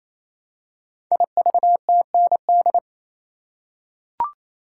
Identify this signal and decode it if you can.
Morse code, heard as a keyed tone, about 29 wpm, 700 Hz. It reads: IVTND
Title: Code Groups - 5 Letters - 28wpm